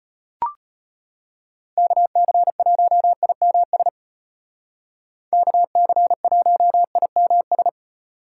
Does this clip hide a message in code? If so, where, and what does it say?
Yes: Morse code from 1.8 to 7.7 s, heard as a keyed tone, about 33 wpm, 700 Hz. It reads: KC1IMS KC1IMS